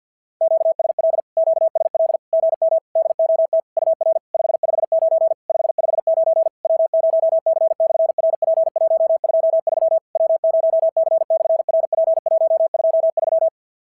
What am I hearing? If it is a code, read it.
Morse code, heard as a keyed tone, about 52 wpm, 690 Hz. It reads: QSL QSL GM DOT UR 559 559 W0C/RP123 W0C/RP123
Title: SOTA QSO Practice 50wpm